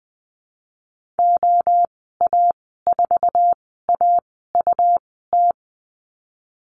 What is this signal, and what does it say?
Morse code, heard as a keyed tone, about 20 wpm, spaced as for 14 wpm, 700 Hz. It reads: OA4AUT
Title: 140 Lesson 23g - Letter f - Call Signs 20wpm